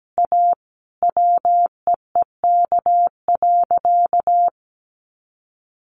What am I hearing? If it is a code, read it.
Morse code, heard as a keyed tone, about 17 wpm, 700 Hz. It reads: A WEEK.
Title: Easy Phrases 17wpm